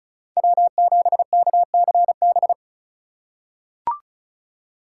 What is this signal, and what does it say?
Morse code, heard as a keyed tone, about 36 wpm, 700 Hz. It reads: W7KCB
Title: USA Call Signs - 5 Characters - 4x Character Spacing - 35wpm